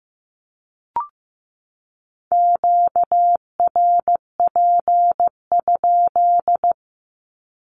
Morse code, heard as a keyed tone, about 15 wpm, 700 Hz. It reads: QRP?
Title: Q-Codes 15wpm